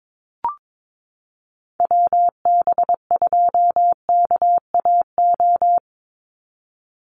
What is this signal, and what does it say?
Morse code, heard as a keyed tone, about 22 wpm, 700 Hz. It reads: WB2KAO
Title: Call Signs Repeated 3x - 22wpm